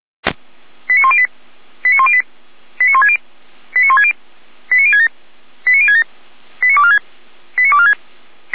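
Five-tone selective call (ZVEI), heard as quick runs of five tones, tones of about 70 ms. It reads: 89108 89108 89160 89160 89067 89067 89246 89246
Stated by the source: Five-tone ZVEI selective call received by SDR, archived by feuerwehr-orlamuende.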